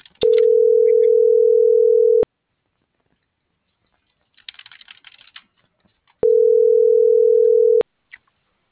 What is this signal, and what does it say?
Ringback tone (North America), 440 + 480 Hz.